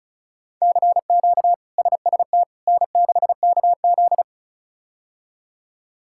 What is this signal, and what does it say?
Morse code, heard as a keyed tone, about 34 wpm, 700 Hz. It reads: CQ SST N6KZ
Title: K1USN SST QSO Practice - 35wpm